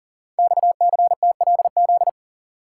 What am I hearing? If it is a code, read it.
Morse code, heard as a keyed tone, about 42 wpm, 700 Hz. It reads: XCTLZ
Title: Code Groups - 5 Letters - 40wpm